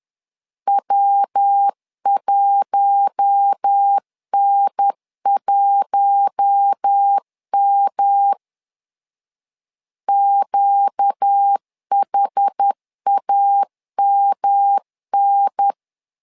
Morse code, heard as a keyed tone, about 10 wpm, 800 Hz. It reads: W1N1M QHAMN
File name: LEZIONI / 4 Q1 10wpm 2448